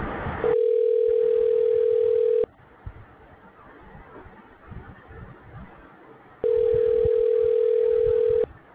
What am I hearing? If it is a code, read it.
Ringback tone (North America), 440 + 480 Hz.